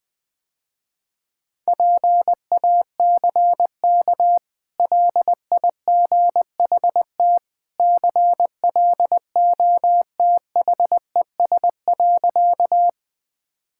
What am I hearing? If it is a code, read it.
Morse code, heard as a keyed tone, about 20 wpm, 700 Hz. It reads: PACK LIGHT CLOTHES.
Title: Easy Sentences with Hints - 20wpm